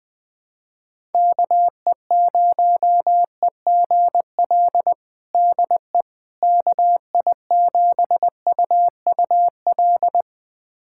Morse code, heard as a keyed tone, about 20 wpm, 700 Hz. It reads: KE0EGL DE KI7UUL
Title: USA Callsign DE Callsign 20wpm